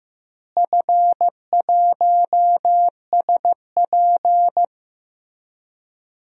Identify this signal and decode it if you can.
Morse code, heard as a keyed tone, about 15 wpm, 700 Hz. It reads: F1SP
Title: DXCC-100 Call Signs - 4 Characters - Repeated 2x - 15wpm